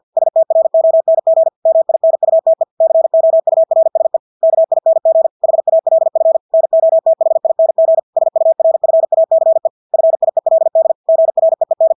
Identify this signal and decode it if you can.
Morse code, heard as a keyed tone, about 50 wpm, 650 Hz. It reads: STRONG MINUTE COURSE KING HALF NOTHING SURFACE FIELD GREEN
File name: Engelske ord 50 wpm